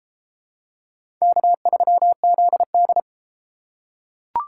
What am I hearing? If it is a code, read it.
Morse code, heard as a keyed tone, about 30 wpm, 700 Hz. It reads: K3ZD